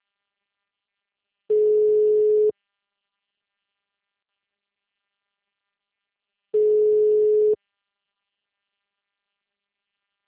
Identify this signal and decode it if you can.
Ringback tone (Europe, 425 Hz).
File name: out- -2014- - - .41807